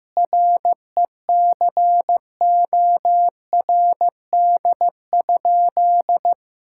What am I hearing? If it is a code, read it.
Morse code, heard as a keyed tone, about 15 wpm, 700 Hz. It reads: RECORD?
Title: Sentences from Top 1000 Words - Review Part 6 - 15wpm